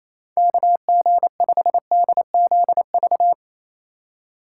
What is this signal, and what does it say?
Morse code, heard as a keyed tone, about 28 wpm, 700 Hz. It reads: KG5DZV
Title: USA Call Signs - 3x Character Spacing - 28wpm